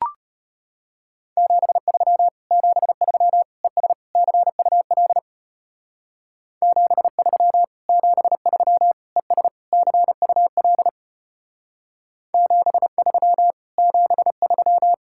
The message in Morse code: 73 73 ES CUL 73 73 ES CUL 73 73

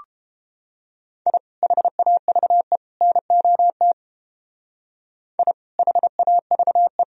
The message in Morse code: I HAVE NOT I HAVE

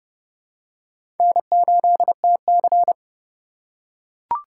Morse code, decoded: N8TC

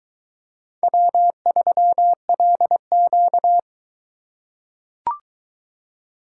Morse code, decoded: W3LQ

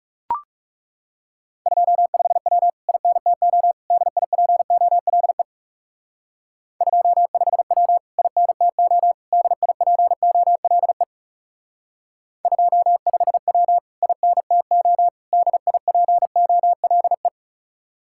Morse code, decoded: 25W INTO DIPOLE 25W INTO DIPOLE 25W INTO DIPOLE

